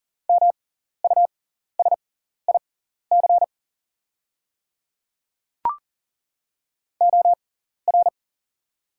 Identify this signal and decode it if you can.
Morse code, heard as a keyed tone, about 39 wpm, spaced as for 14 wpm, 690 Hz. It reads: MUSIC OR
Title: 6x Character Spacing - Top 500 Words - 40wpm